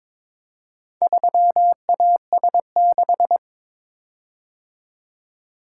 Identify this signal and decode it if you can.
Morse code, heard as a keyed tone, about 22 wpm, 700 Hz. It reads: 3AS6
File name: practice / 22wpm / Four Letter-Number Combinations - 22wpm